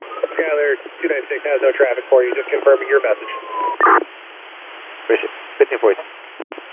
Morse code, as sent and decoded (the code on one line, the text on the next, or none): none